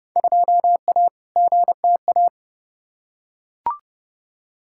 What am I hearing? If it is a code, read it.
Morse code, heard as a keyed tone, about 30 wpm, 700 Hz. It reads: 2A GTA
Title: Field Day Exchanges Encore - 30wpm